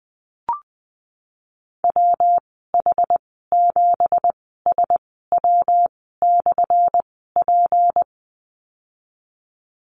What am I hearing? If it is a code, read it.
Morse code, heard as a keyed tone, about 20 wpm, spaced as for 15 wpm, 700 Hz. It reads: WH7SW/P
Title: 179 Lesson 29g - Slash - Call Signs 20wpm